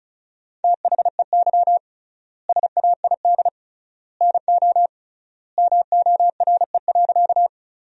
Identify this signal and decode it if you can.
Morse code, heard as a keyed tone, about 34 wpm, 700 Hz. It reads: THEY SAID NO MORE.
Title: 3x Word-Spacing - Sentences from Top 100 Words - 35wpm